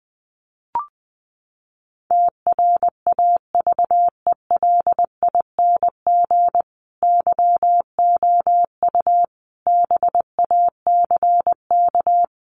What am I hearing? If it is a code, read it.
Morse code, heard as a keyed tone, about 20 wpm, 700 Hz. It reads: TRAVELING YOU BACK
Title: A Sound Of Thunder 20wpm